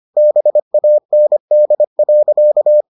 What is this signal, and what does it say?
Morse code, heard as a keyed tone, about 25 wpm, 600 Hz. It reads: BAND.